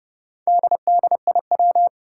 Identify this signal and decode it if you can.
Morse code, heard as a keyed tone, about 30 wpm, 700 Hz. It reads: DDIW
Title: Four Challenging Letter-Number Combinations 30wpm